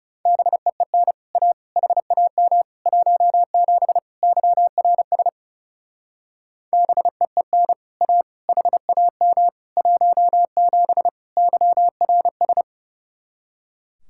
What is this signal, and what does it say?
Morse code, heard as a keyed tone, about 37 wpm, 700 Hz. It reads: BEEN A HAM 17 YRS BEEN A HAM 17 YRS
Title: Speed-Racing Common QSO Phrases Encore - 30wpm